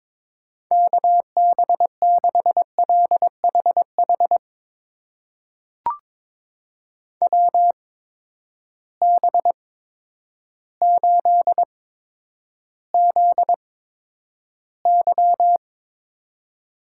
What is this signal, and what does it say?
Morse code, heard as a keyed tone, about 21 wpm, spaced as for 10 wpm, 700 Hz. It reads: KB6LHH WB8ZY
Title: USA Call Signs - 6 Characters - 8x Character Spacing - 22wpm